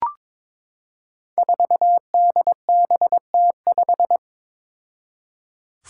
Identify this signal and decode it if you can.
Morse code, heard as a keyed tone, about 22 wpm, 700 Hz. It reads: 4DBT5
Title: Five Challenging Letter-Number Combinations 22wpm